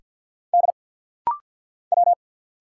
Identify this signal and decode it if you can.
Morse code, heard as a keyed tone, about 50 wpm, 700 Hz. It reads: D W
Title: Single Letter-Number - Rapid-Fire 50wpm